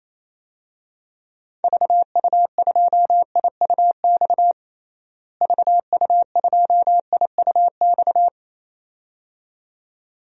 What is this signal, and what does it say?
Morse code, heard as a keyed tone, about 27 wpm, 700 Hz. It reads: VU2IUX VU2IUX
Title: DXCC-100 Call Signs - 6 Characters - Repeated 2x - 28wpm